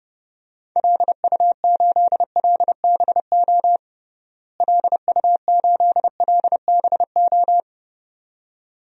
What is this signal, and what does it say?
Morse code, heard as a keyed tone, about 30 wpm, 700 Hz. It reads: LU8LBO LU8LBO
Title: DXCC-100 Call Signs - 6 Characters - Repeated 2x - 30wpm